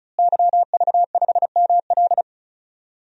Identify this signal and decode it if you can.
Morse code, heard as a keyed tone, about 36 wpm, 700 Hz. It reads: YV5ML
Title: DXCC-100 Call Signs - 5 Characters - Repeated 3x - 35wpm